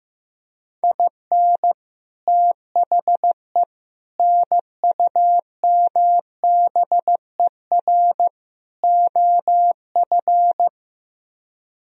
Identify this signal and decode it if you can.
Morse code, heard as a keyed tone, about 15 wpm, 700 Hz. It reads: IN THE NUMBER OF
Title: Speed-Racing - Sets of 4 Words 15wpm